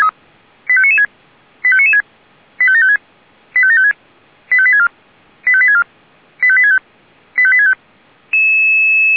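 Five-tone selective call (ZVEI), heard as quick runs of five tones, tones of about 70 ms. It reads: 85996 85996 85756 85756 85864 85864 85865 85865
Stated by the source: ZVEI five-tone selective call received with an SDR.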